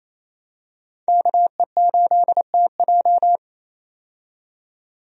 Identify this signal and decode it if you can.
Morse code, heard as a keyed tone, about 27 wpm, 700 Hz. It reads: KE8TJ
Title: USA Call Signs - 5 Characters - 28wpm